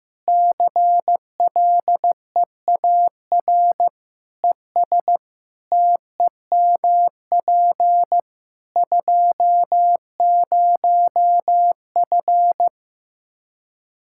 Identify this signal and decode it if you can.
Morse code, heard as a keyed tone, about 15 wpm, 700 Hz. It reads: CLEAR ES TEMP 20F
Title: K1USN SST QSO Practice 15wpm